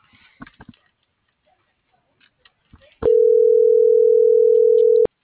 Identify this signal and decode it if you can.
Ringback tone (North America), 440 + 480 Hz.